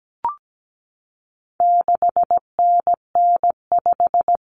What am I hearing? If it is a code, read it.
Morse code, heard as a keyed tone, about 17 wpm, 700 Hz. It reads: 6NN5
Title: Common Cut Numbers - 4-Digit 17wpm